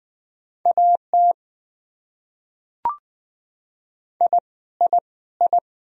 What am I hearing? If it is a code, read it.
Morse code, heard as a keyed tone, about 20 wpm, 700 Hz. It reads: AT I I I
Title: Top 100 Words Repeated 20wpm